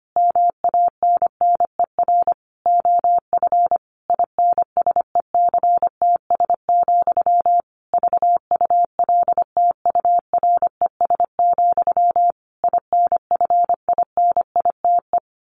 Morse code, heard as a keyed tone, about 25 wpm, 700 Hz. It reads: MANNER OF INSECTS, VULTURES, INFINITE
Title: A Sound Of Thunder 25wpm